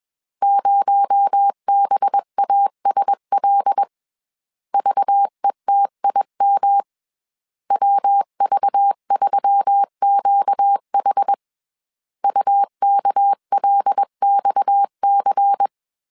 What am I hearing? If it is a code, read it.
Morse code, heard as a keyed tone, about 21 wpm, 800 Hz. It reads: 0BASL VETIM WV3QH UKLXC
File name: CW / DET / 1 6 1500C S B 20wpm 2048